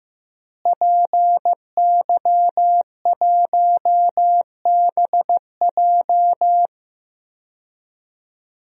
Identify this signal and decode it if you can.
Morse code, heard as a keyed tone, about 15 wpm, 700 Hz. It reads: PY1BJ
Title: DXCC-100 Call Signs - 5 Characters - Repeated 3x - 15wpm